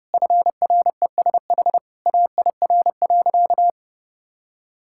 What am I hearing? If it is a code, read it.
Morse code, heard as a keyed tone, about 32 wpm, 700 Hz. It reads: FRESH AIR.
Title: Sentences from Top 1600 Words 30wpm